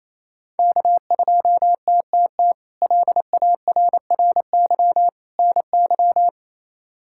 Morse code, heard as a keyed tone, about 28 wpm, 700 Hz. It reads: K2TTT LARRY NY
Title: K1USN SST QSO Practice - 28wpm